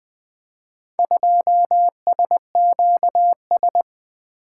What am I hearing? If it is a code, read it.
Morse code, heard as a keyed tone, about 20 wpm, 700 Hz. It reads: 2SQS
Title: Four Letter-Number Combinations - 20wpm